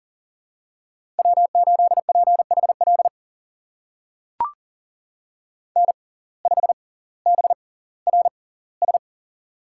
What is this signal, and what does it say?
Morse code, heard as a keyed tone, about 40 wpm, spaced as for 20 wpm, 700 Hz. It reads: W8PHL N5BRS
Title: USA Call Signs - 5 Characters - 6x Character Spacing - 40wpm